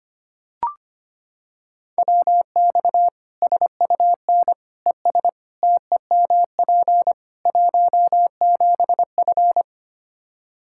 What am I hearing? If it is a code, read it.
Morse code, heard as a keyed tone, about 25 wpm, 700 Hz. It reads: WX SUN ES TEMP 17F
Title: Practice QSO - Exchange 2 - Info - 25wpm